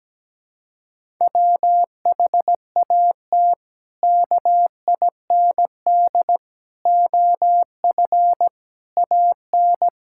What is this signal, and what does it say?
Morse code, heard as a keyed tone, about 17 wpm, 700 Hz. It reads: WHAT KIND OF AN
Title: Sentences from Top 1000 Words - Review Part 5 - 17wpm